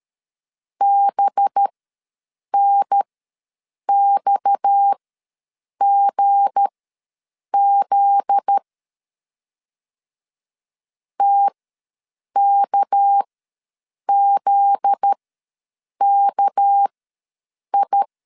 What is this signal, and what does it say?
Morse code, heard as a keyed tone, about 13 wpm, spaced as for 7 wpm, 800 Hz. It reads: BNXGZ TKZKI
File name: 7 GF9Z ws 12wpm